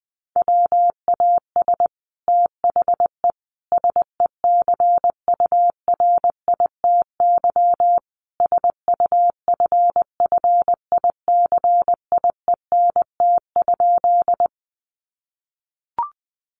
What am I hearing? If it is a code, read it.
Morse code, heard as a keyed tone, about 19 wpm, 700 Hz. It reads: WAS THE SECURITY SUFFICIENT?
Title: Sentences from Top 1400 Words 20wpm